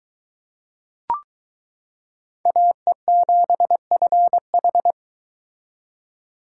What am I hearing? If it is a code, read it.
Morse code, heard as a keyed tone, about 22 wpm, 700 Hz. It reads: AE7FH